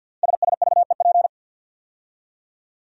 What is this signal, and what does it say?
Morse code, heard as a keyed tone, about 50 wpm, 700 Hz. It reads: SSVEP
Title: Code Groups - 5 Letters - 50wpm